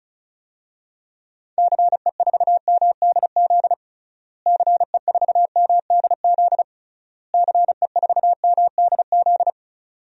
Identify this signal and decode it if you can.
Morse code, heard as a keyed tone, about 36 wpm, 700 Hz. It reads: CE4MDZ CE4MDZ CE4MDZ
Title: DXCC-100 Call Signs - 6 Characters - Repeated 3x - 35wpm